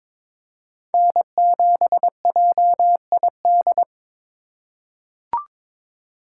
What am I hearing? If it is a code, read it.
Morse code, heard as a keyed tone, about 22 wpm, 700 Hz. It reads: N7JID
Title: USA Call Signs - 5 Characters - 2x Character Spacing - 22wpm